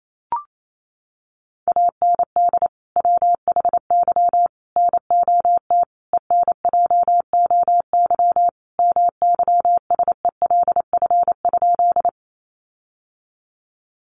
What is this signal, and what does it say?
Morse code, heard as a keyed tone, about 28 wpm, 700 Hz. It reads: AND WHY NOT ENJOY MYSELF?
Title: Sentences from Top 1300 Words 28wpm